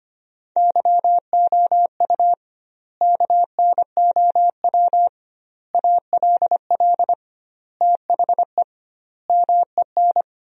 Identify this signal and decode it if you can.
Morse code, heard as a keyed tone, about 25 wpm, 700 Hz. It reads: YOU KNOW ALL THE MEN
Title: 2x Word-Spacing - Sentences from Top 300 Words - 25wpm